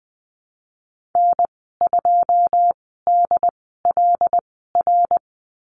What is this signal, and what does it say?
Morse code, heard as a keyed tone, about 19 wpm, spaced as for 14 wpm, 700 Hz. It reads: N2DLR
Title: 082 Lesson 14g - Number 2 - Call Signs 20wpm